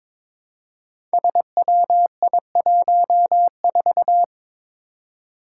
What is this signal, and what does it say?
Morse code, heard as a keyed tone, about 22 wpm, 700 Hz. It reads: SWI14
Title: Five Challenging Letter-Number Combinations 22wpm